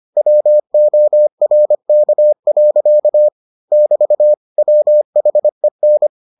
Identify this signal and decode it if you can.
Morse code, heard as a keyed tone, about 25 wpm, 600 Hz. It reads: WORK. <BT> WHEN